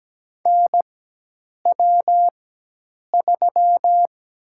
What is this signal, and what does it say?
Morse code, heard as a keyed tone, about 17 wpm, 700 Hz. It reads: N W 3
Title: USA Call Signs - 4 Characters - 4x Character Spacing - 17wpm